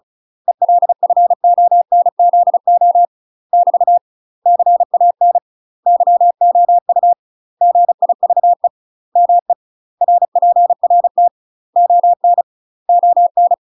Morse code, heard as a keyed tone, about 34 wpm, 700 Hz. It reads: ELFONZO <BT> CAN YOU GIVE ME RPRT ON ON
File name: UK MASTER QSOs TRAINING / 35WPM / UK MASTER QSOs 0036